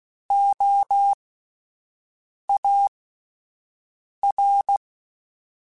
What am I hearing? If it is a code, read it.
Morse code, heard as a keyed tone, about 16 wpm, spaced as for 5 wpm, 800 Hz. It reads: OAR
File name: Alphabetical / Alphabetical mp3s / Os @5